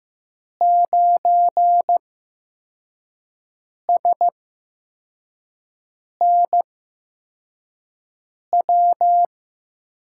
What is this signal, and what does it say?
Morse code, heard as a keyed tone, about 15 wpm, spaced as for 4 wpm, 700 Hz. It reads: 9SNW
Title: USA Call Signs - 8x Character Spacing - 15wpm